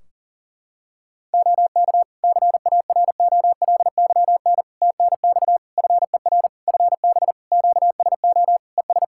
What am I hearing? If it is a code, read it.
Morse code, heard as a keyed tone, about 40 wpm, 700 Hz. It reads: OK CAROLYN TNX FER FB QSO ES
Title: Practice QSO - Exchange 4 - Ending - 40wpm